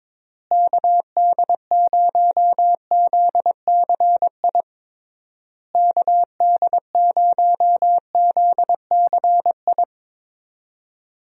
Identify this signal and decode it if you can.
Morse code, heard as a keyed tone, about 22 wpm, 700 Hz. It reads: KD0ZCI KD0ZCI
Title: USA Call Signs - 6 Characters - Repeated 3x - 22wpm